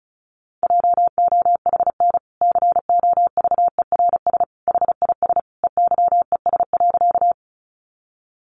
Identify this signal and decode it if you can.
Morse code, heard as a keyed tone, about 34 wpm, 700 Hz. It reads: JOHN COVERS HIS EYES.
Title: Easy Phrases 35wpm